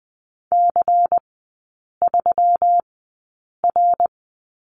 Morse code, heard as a keyed tone, about 20 wpm, spaced as for 8 wpm, 700 Hz. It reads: C3R
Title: 121 Lesson 20e - Number 3 - Three Characters 20wpm